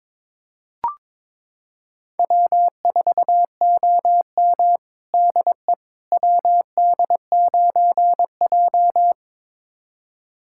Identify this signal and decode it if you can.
Morse code, heard as a keyed tone, about 22 wpm, 700 Hz. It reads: W4OM DE WD9J
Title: USA Callsign DE Callsign - 4 Characters - 22wpm